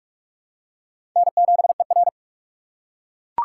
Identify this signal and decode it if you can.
Morse code, heard as a keyed tone, about 46 wpm, 700 Hz. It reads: N7ER
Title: USA Call Signs - 4 Characters - 6x Character Spacing - 45wpm